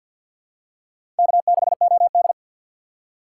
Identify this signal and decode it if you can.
Morse code, heard as a keyed tone, about 47 wpm, 700 Hz. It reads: K6OD
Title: USA Call Signs - 4 Characters - 3x Character Spacing - 50wpm